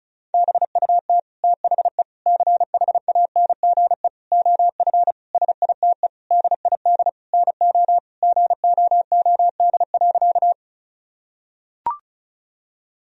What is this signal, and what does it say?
Morse code, heard as a keyed tone, about 34 wpm, 700 Hz. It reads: BUT THE CHANGE OF SITE DID NO GOOD.
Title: Sentences from Top 1000 Words - Review Part 4 - 35wpm